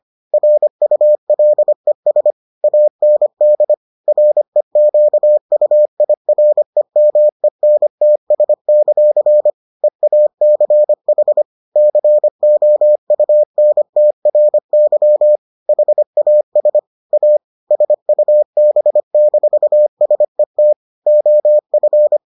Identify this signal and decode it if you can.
Morse code, heard as a keyed tone, about 25 wpm, 600 Hz. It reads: RULES AND REQUIREMENTS; EACH COUNTRY HAS A SUB-SET OF